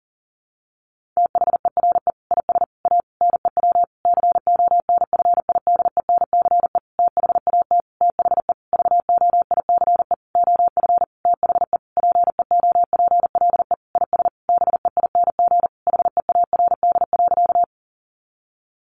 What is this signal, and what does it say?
Morse code, heard as a keyed tone, about 40 wpm, 700 Hz. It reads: THERE IS A NEW CONFIDENCE THAT THE VOICE OF THE PEOPLE IS BEING HEARD.
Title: Sentences from Top 1500 Words 40wpm